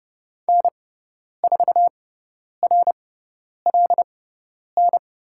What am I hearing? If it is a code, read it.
Morse code, heard as a keyed tone, about 30 wpm, spaced as for 10 wpm, 700 Hz. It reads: N4RLN
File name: Lesson 18g - Letter m - Call Signs 30wpm